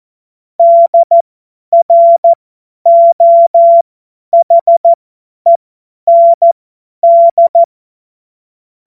Morse code, heard as a keyed tone, about 14 wpm, spaced as for 10 wpm, 680 Hz. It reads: DROHEND
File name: Morsekurs / Witze / Witz 15. August 2023 50 F70